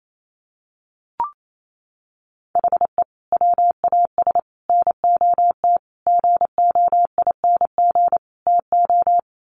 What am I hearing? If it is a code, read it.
Morse code, heard as a keyed tone, about 28 wpm, 700 Hz. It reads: HE WAS NOT GOING TO
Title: Sets of 5 Words 28wpm